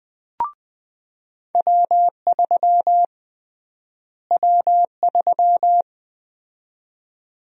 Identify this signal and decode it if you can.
Morse code, heard as a keyed tone, about 21 wpm, 700 Hz. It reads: W3 W3
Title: USA Call Sign Prefixes - Repeated 2x - 20wpm